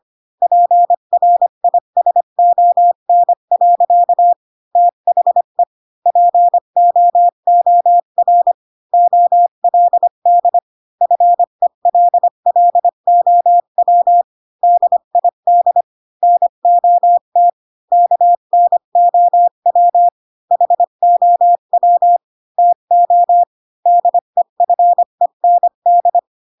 Morse code, heard as a keyed tone, about 25 wpm, 700 Hz. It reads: PRISON. THE POOR OLD FELLOW DID NOT KNOW HOW TO DEFEND
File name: Pinocchio / Pinocchio ch3 20 25 ik1hge